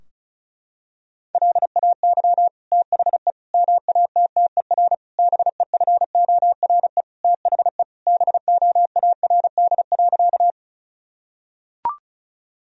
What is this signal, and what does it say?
Morse code, heard as a keyed tone, about 34 wpm, 700 Hz. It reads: LAY THE MATTER BEFORE THE BOARD.
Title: Sentences from Top 1000 Words - Review Part 4 - 35wpm